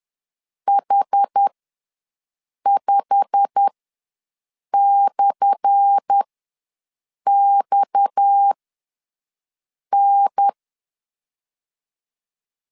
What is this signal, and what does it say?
Morse code, heard as a keyed tone, about 10 wpm, spaced as for 5 wpm, 800 Hz. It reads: H5/XN
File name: LEZIONI / 10 SIGNS / 10A SLASH ws 10wpm 4121